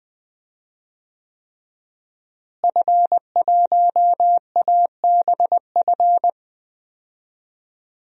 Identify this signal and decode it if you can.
Morse code, heard as a keyed tone, about 20 wpm, 700 Hz. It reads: F1ABF
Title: DXCC-100 Call Signs - 5 Characters - Repeated 3x - 20wpm